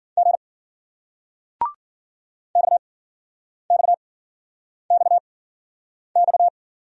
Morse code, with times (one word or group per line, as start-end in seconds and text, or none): 0.2-0.4 G
2.6-6.5 XXXX